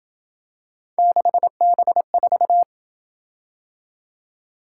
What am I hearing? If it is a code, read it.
Morse code, heard as a keyed tone, about 27 wpm, 700 Hz. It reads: 6B4